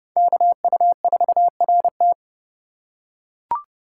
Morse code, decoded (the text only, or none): KU4RT